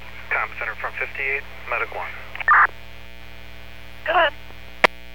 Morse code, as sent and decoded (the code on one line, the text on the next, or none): none